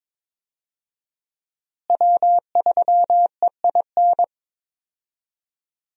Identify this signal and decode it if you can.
Morse code, heard as a keyed tone, about 21 wpm, 700 Hz. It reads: W3EIN
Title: USA Call Signs - 5 Characters - Repeated 3x - 22wpm